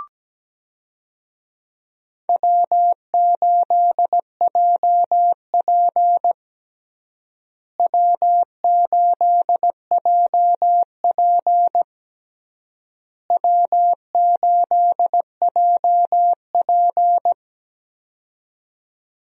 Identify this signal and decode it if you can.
Morse code, heard as a keyed tone, about 17 wpm, 700 Hz. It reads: W8JP W8JP W8JP
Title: USA Call Signs - 4 Characters - Repeated 3x - 17wpm